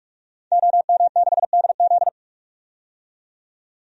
Morse code, heard as a keyed tone, about 45 wpm, 700 Hz. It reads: OM6DZ
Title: DXCC-100 Call Signs - 5 Characters - 45wpm